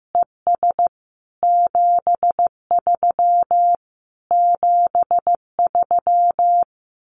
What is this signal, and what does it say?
Morse code, heard as a keyed tone, about 15 wpm, 700 Hz. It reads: ES 73 73
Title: Common QSO Phrases Encore 15wpm